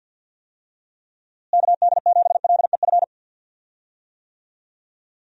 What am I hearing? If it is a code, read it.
Morse code, heard as a keyed tone, about 52 wpm, 700 Hz. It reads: KD7LEF
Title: USA Call Signs - 6 Characters - Repeated 2x - 50wpm